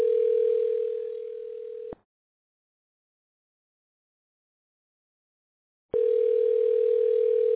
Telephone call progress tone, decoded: ringback tone